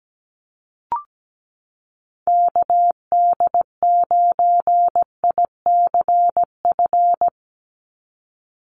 Morse code, heard as a keyed tone, about 17 wpm, 700 Hz. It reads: KD9ICF
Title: USA Call Signs 17wpm